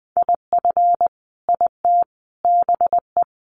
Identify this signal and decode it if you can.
Morse code, heard as a keyed tone, about 20 wpm, 700 Hz. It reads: IF IT BE
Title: Sentences from Top 1300 Words 20wpm